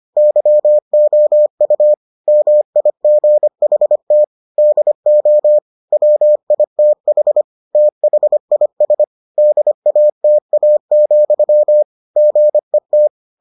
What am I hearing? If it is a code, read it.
Morse code, heard as a keyed tone, about 25 wpm, 600 Hz. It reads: YOU MIGHT DO WITH THIS DATA, GET